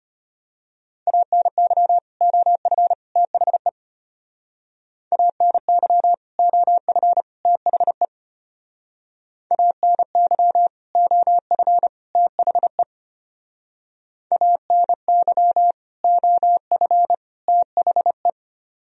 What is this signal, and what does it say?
Morse code, heard as a keyed tone, about 30 wpm, 700 Hz. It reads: ANY OF THE ANY OF THE ANY OF THE ANY OF THE
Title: Speed-Racing - Sets of 3 Words 25wpm